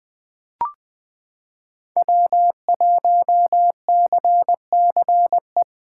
Morse code, heard as a keyed tone, about 20 wpm, 700 Hz. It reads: W1CCE
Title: K1USN SST QSO Practice - 20wpm